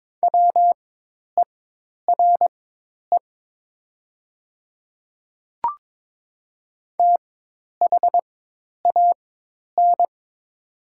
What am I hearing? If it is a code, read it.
Morse code, heard as a keyed tone, about 22 wpm, spaced as for 10 wpm, 700 Hz. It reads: WERE THAN
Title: 4x Character Spacing - Top 100 Words - 22wpm